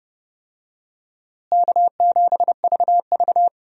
Morse code, heard as a keyed tone, about 30 wpm, 700 Hz. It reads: K7VV
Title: USA Call Signs - 4 Characters - Repeated 3x - 30wpm